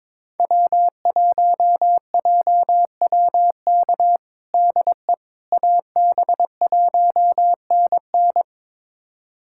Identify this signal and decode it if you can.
Morse code, heard as a keyed tone, about 22 wpm, 700 Hz. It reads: W1JWK DE AB1NN
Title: USA Callsign DE Callsign - 5 Characters - 22wpm